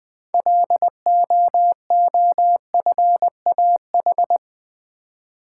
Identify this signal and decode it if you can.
Morse code, heard as a keyed tone, about 20 wpm, 700 Hz. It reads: LOOFAH
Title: Difficult Words with Hints - 20wpm